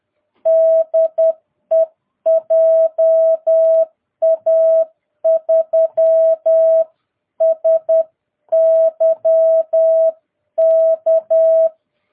Morse code, heard as a keyed tone, about 10 wpm, 650 Hz. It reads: DE JA3SYK
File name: jh8cbh / 2022cw